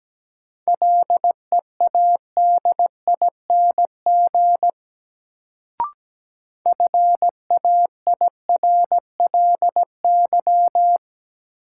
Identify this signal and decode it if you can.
Morse code, heard as a keyed tone, about 17 wpm, 700 Hz. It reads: LEADING FAIRLY
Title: Sentences from Top 1100 Words - New Word Drill Alt - 17wpm